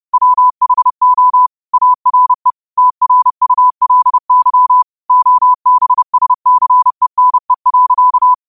Morse code, heard as a keyed tone, about 30 wpm, 1000 Hz. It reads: WHO ARE TRULY OBSCENE.
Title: 30 WPM morse code quotes for Fri, 01 Mar 2024